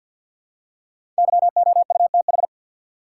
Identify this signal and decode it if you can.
Morse code, heard as a keyed tone, about 49 wpm, 700 Hz. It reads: YOUTH
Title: Five Letter Words - 50wpm